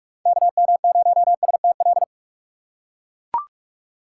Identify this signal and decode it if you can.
Morse code, heard as a keyed tone, about 41 wpm, 700 Hz. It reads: KM0STL
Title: USA Call Signs - 6 Characters - 3x Character Spacing - 45wpm